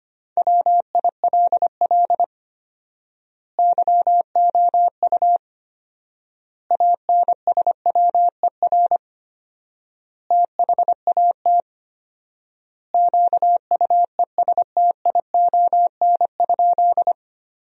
Morse code, heard as a keyed tone, about 25 wpm, 700 Hz. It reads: WILL YOU ANSWER THAT QUESTION?
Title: 4x Word-Spacing - Sentences from Top 400 Words - 25wpm